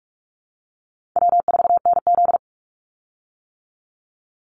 Morse code, heard as a keyed tone, about 46 wpm, 700 Hz. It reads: W4NZ
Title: Easy Call Signs - 45wpm